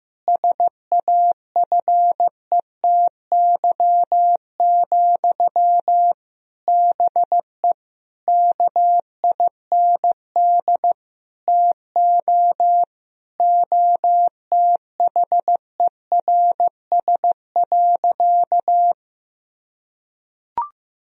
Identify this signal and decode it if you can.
Morse code, heard as a keyed tone, about 16 wpm, 700 Hz. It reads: SAFETY, BE KIND TO OTHERS.
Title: Sentences from Top 1000 Words - Review Part 4 - 15wpm